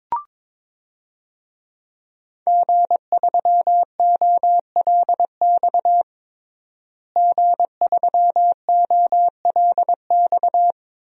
Morse code, heard as a keyed tone, about 22 wpm, 700 Hz. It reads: G3OLX G3OLX
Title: DXCC-100 Call Signs - 5 Characters - Repeated 3x - 22wpm